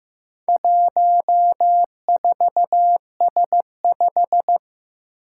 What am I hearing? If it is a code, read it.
Morse code, heard as a keyed tone, about 15 wpm, 700 Hz. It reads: 14S5
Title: Four Challenging Letter-Number Combinations 15wpm